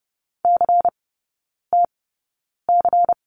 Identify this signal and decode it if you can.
Morse code, heard as a keyed tone, about 30 wpm, spaced as for 9 wpm, 700 Hz. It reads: CTC
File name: Lesson 18e - Letter m - Three Characters 30wpm